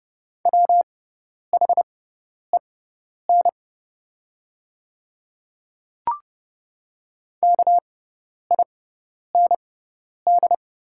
Morse code, heard as a keyed tone, about 29 wpm, spaced as for 10 wpm, 690 Hz. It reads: WHEN KIND
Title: 6x Character Spacing - Top 200 Words - 30wpm